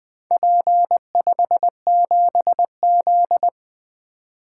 Morse code, heard as a keyed tone, about 20 wpm, 700 Hz. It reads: P57Z
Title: Four Letter-Number Combinations - 20wpm